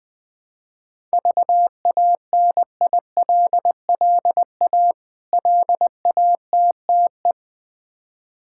Morse code, heard as a keyed tone, about 20 wpm, 700 Hz. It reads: VANILLA LATTE
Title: Difficult Words with Hints - 20wpm